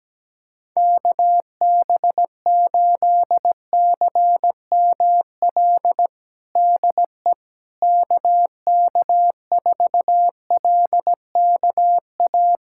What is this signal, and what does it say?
Morse code, heard as a keyed tone, about 17 wpm, 700 Hz. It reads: KB8CML DE KK4LKA
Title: USA Callsign DE Callsign 17wpm